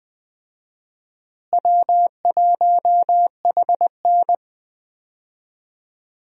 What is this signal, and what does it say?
Morse code, heard as a keyed tone, about 21 wpm, 700 Hz. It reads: W1HN
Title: USA Call Signs - 4 Characters - Repeated 3x - 20wpm